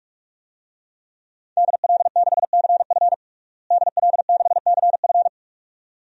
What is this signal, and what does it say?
Morse code, heard as a keyed tone, about 46 wpm, 700 Hz. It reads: DL6CF DL6CF
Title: DXCC-100 Call Signs - 5 Characters - Repeated 2x - 45wpm